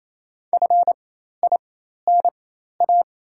Morse code, heard as a keyed tone, about 28 wpm, spaced as for 13 wpm, 700 Hz. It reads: FINA
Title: 4x Character Spacing - Top 500 Words - 28wpm